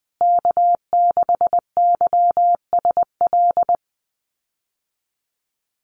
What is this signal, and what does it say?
Morse code, heard as a keyed tone, about 20 wpm, 700 Hz. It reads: K6YSL
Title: USA Call Signs Repeated 2x 20wpm